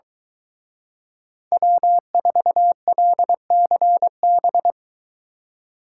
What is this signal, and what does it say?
Morse code, heard as a keyed tone, about 23 wpm, 700 Hz. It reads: W4LCB